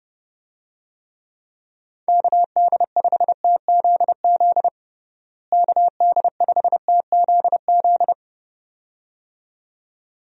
Morse code, heard as a keyed tone, about 32 wpm, 700 Hz. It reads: KD5TZZ KD5TZZ
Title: USA Call Signs - 6 Characters - Repeated 2x - 30wpm